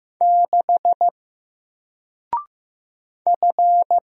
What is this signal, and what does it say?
Morse code, heard as a keyed tone, about 15 wpm, 700 Hz. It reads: 6 F
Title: Single Letter-Number Alt 15wpm